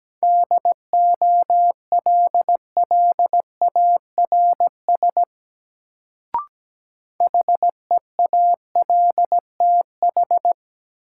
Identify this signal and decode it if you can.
Morse code, heard as a keyed tone, about 17 wpm, 700 Hz. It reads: DOLLARS HEALTH
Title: Sets of 5 Words - New Word Drill - 17wpm